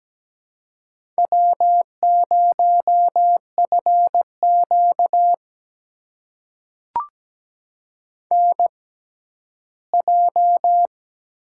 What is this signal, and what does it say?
Morse code, heard as a keyed tone, about 17 wpm, spaced as for 11 wpm, 700 Hz. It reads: W0FQ NJ
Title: USA Call Signs - 4 Characters - 6x Character Spacing - 17wpm